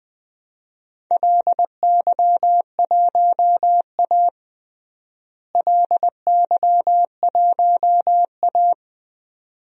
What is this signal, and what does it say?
Morse code, heard as a keyed tone, about 20 wpm, 700 Hz. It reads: LY1A LY1A
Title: DXCC-100 Call Signs - 4 Characters - Repeated 2x - 20wpm